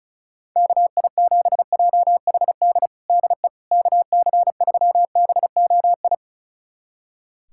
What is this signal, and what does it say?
Morse code, heard as a keyed tone, about 34 wpm, 700 Hz. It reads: KI7JHD DE KC3BOI
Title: USA Callsign DE Callsign 35wpm